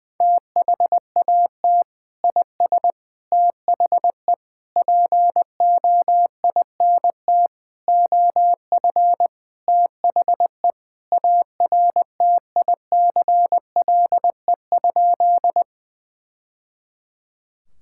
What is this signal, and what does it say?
Morse code, heard as a keyed tone, about 20 wpm, 700 Hz. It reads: THAT IS THE POINT OF THE ARTICLE?
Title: Sentences from Top 1000 Words - Review Part 5 - 20wpm